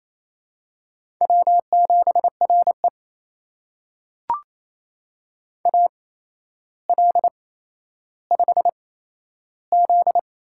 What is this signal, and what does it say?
Morse code, heard as a keyed tone, about 27 wpm, spaced as for 12 wpm, 700 Hz. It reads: W7RE AL5Z
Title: USA Call Signs - 4 Characters - 8x Character Spacing - 28wpm